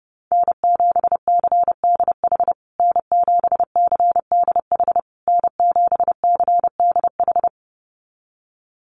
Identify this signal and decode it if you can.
Morse code, heard as a keyed tone, about 30 wpm, 700 Hz. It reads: N7CDH N7CDH N7CDH
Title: USA Call Signs Repeated 3x 30wpm